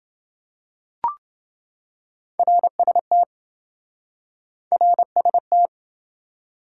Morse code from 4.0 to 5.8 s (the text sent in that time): RST